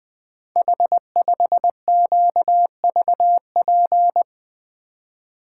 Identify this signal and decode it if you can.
Morse code, heard as a keyed tone, about 20 wpm, 700 Hz. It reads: H5QVP
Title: Code Groups - 5 Letter:Numbers - 20wpm